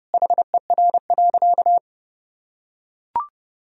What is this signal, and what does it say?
Morse code, heard as a keyed tone, about 32 wpm, 700 Hz. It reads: HER.
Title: Sentences from Top 1000 Words - Review Part 2 - 30wpm